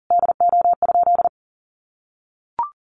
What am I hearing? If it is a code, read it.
Morse code, heard as a keyed tone, about 40 wpm, 700 Hz. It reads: DO?
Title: Sentences from Top 100 Words 40wpm